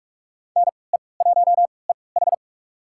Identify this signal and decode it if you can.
Morse code, heard as a keyed tone, about 43 wpm, spaced as for 25 wpm, 700 Hz. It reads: NE1EH
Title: USA Call Signs - 5 Characters - 3x Character Spacing - 45wpm